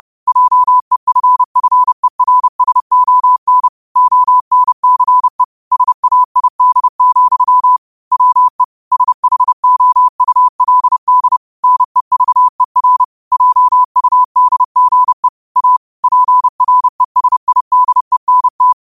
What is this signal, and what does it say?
Morse code, heard as a keyed tone, about 30 wpm, 1000 Hz. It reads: JEFFERSON ONCE SAID, WE SHOULD NEVER JUDGE A PRESIDENT
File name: qotdfiles / 30wpm